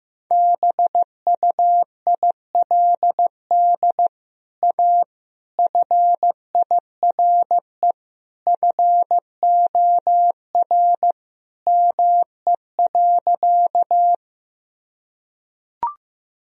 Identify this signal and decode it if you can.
Morse code, heard as a keyed tone, about 15 wpm, 700 Hz. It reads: BUILD A FIRE FOR ME.
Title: Sentences from Top 1000 Words - Review Part 1 - 15wpm